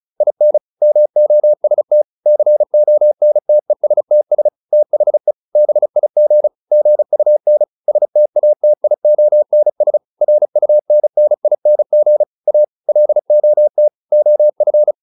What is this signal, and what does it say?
Morse code, heard as a keyed tone, about 34 wpm, 600 Hz. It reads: IN MOST CONTESTS THE BIG GUN STATIONS RUNNING A LOT OF